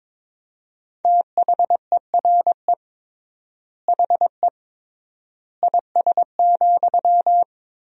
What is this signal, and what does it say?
Morse code, heard as a keyed tone, about 22 wpm, 700 Hz. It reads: THERE HE IS,
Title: 3x Word-Spacing - Sentences from Top 100 Words - 22wpm